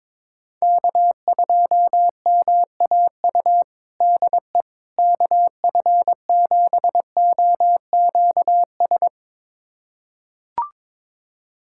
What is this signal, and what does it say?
Morse code, heard as a keyed tone, about 22 wpm, 700 Hz. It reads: K2MAU DE KF7OQS
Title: USA Callsign DE Callsign 22wpm